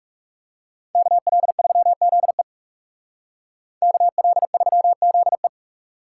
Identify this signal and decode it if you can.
Morse code, heard as a keyed tone, about 40 wpm, 700 Hz. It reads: KL3ZE KL3ZE